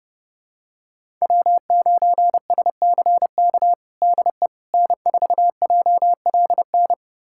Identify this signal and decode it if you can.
Morse code, heard as a keyed tone, about 30 wpm, 700 Hz. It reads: W9SCK DE N4JLN
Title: USA Callsign DE Callsign - 5 Characters - 30wpm